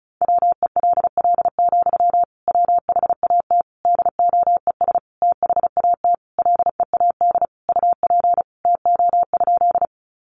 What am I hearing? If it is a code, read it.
Morse code, heard as a keyed tone, about 34 wpm, 700 Hz. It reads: WELL, WHAT DOES THAT LEAD UP TO?
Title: Sentences from Top 500 Words 35wpm